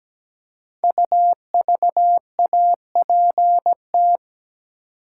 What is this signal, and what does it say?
Morse code, heard as a keyed tone, about 17 wpm, 700 Hz. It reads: UVAPT
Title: Code Groups - 5 Letter:Numbers - 17wpm